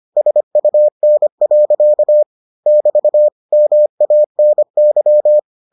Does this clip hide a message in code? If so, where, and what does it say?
Yes: Morse code from 0.2 to 5.4 s, heard as a keyed tone, about 25 wpm, 600 Hz. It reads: SUN. <BT> MANY